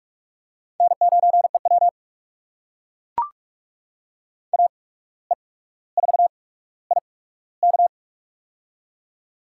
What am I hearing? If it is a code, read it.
Morse code, heard as a keyed tone, about 44 wpm, spaced as for 18 wpm, 700 Hz. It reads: N9EW AE4IK
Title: USA Call Signs - 8x Character Spacing - 45wpm